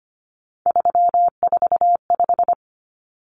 Morse code, heard as a keyed tone, about 25 wpm, 700 Hz. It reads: 345
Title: Numbers - 3-Digit 25wpm